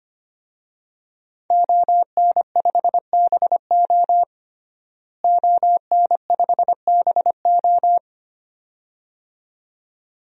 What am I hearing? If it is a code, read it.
Morse code, heard as a keyed tone, about 25 wpm, 700 Hz. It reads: ON5BO ON5BO
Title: DXCC-100 Call Signs - 5 Characters - Repeated 2x - 25wpm